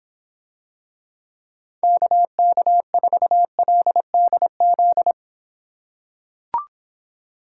Morse code, heard as a keyed tone, about 26 wpm, 700 Hz. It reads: KK4LDZ